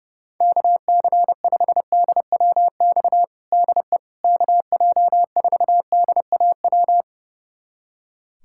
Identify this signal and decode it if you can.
Morse code, heard as a keyed tone, about 32 wpm, 700 Hz. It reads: KC5DWX DE KJ4DAW
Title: USA Callsign DE Callsign - 6 Characters - 30wpm